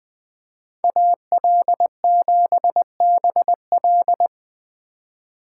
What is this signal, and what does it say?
Morse code, heard as a keyed tone, about 20 wpm, 700 Hz. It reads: AL7BL
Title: USA Call Signs - 5 Characters - 4x Character Spacing - 20wpm